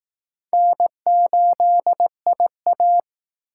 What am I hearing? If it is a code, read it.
Morse code, heard as a keyed tone, about 19 wpm, 700 Hz. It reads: N8IA
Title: Speed Racing - USA 1x2 Call Signs 15wpm